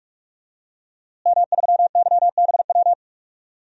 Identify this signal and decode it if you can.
Morse code, heard as a keyed tone, about 48 wpm, 700 Hz. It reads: M3YBW
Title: DXCC-100 Call Signs - 5 Characters - Repeated 3x - 45wpm